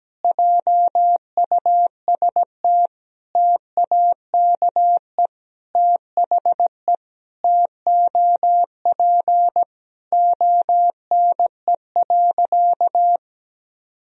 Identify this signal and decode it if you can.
Morse code, heard as a keyed tone, about 17 wpm, 700 Hz. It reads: JUST TAKE THE TOP ONE.
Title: Sentences from Top 1000 Words - Review Part 2 - 17wpm